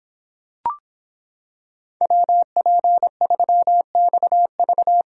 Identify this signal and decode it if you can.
Morse code, heard as a keyed tone, about 25 wpm, 700 Hz. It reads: WP3XV